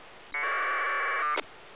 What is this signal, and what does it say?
Packet radio (AX.25, as in APRS).